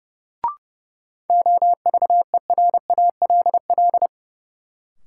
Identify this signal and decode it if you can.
Morse code, heard as a keyed tone, about 29 wpm, 700 Hz. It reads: OVERALL
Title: Sentences from Top 1000 Words - Review Part 7 - New Word Drill - 30wpm